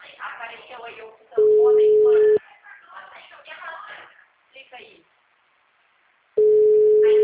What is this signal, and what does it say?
Ringback tone (Europe, 425 Hz), with speech.
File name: out- -2017- - - .5364